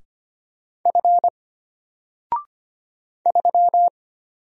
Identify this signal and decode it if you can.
Morse code, heard as a keyed tone, about 25 wpm, 700 Hz. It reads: F 3
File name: Single-Letter-Number-25wpm